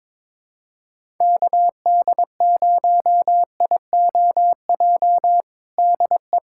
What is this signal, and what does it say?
Morse code, heard as a keyed tone, about 22 wpm, 700 Hz. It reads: KD0IOJ DE
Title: USA Callsign DE Callsign 22wpm